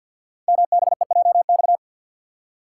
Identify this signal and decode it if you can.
Morse code, heard as a keyed tone, about 50 wpm, 700 Hz. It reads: MBEJX
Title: Code Groups - 5 Letters - 50wpm